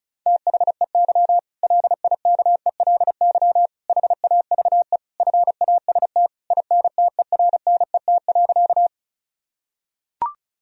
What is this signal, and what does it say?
Morse code, heard as a keyed tone, about 36 wpm, 700 Hz. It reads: THEY LIKELY HAVE FAST INTERNET.
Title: Sentences from Top 1000 Words - Review Part 4 - 35wpm